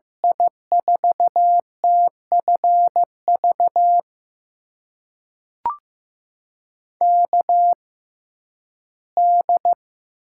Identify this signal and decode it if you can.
Morse code, heard as a keyed tone, about 15 wpm, spaced as for 11 wpm, 700 Hz. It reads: I4TFV KD
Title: USA Call Signs - 6 Characters - 6x Character Spacing - 15wpm